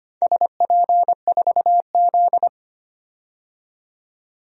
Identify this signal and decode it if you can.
Morse code, heard as a keyed tone, about 25 wpm, 700 Hz. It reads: SP4Z
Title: DXCC-100 Call Signs - 4 Characters - 25wpm